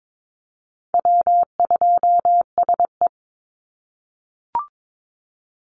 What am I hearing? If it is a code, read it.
Morse code, heard as a keyed tone, about 22 wpm, 700 Hz. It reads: W2SE
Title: NA SOTA Chaser Callsigns 22wpm